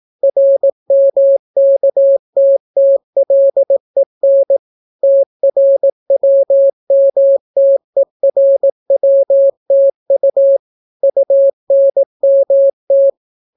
Morse code, heard as a keyed tone, about 18 wpm, 550 Hz. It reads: RMKTTLEN TRWMTERWTU UNMT